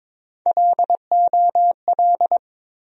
Morse code, heard as a keyed tone, about 22 wpm, 700 Hz. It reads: LOL
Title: USA Call Sign Postfixes - 3 Letters 22wpm